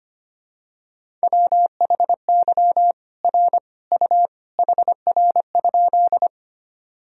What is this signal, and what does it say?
Morse code, heard as a keyed tone, about 25 wpm, 700 Hz. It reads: WHY R U HR?
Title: Sentences from Top 200 Words and Abbreviations 25wpm